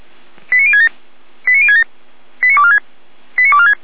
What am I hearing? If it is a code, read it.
Five-tone selective call (ZVEI), heard as quick runs of five tones, tones of about 70 ms. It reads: 89067 89067 89246 89246
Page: SDR/ZVEI